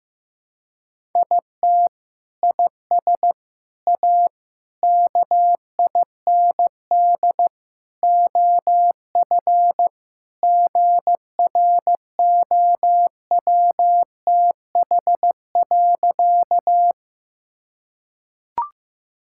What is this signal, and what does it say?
Morse code, heard as a keyed tone, about 15 wpm, 700 Hz. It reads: IT IS A KIND OF GROWTH.
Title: Sentences from Top 1000 Words - Review Part 5 - 15wpm